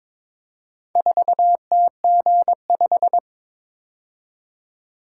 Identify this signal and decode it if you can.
Morse code, heard as a keyed tone, about 22 wpm, 700 Hz. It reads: 4TG5
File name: practice / 22wpm / Four Letter-Number Combinations - 22wpm